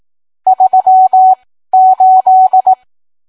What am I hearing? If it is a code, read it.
Morse code, heard as a keyed tone, about 18 wpm, spaced as for 13 wpm, 750 Hz. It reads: 38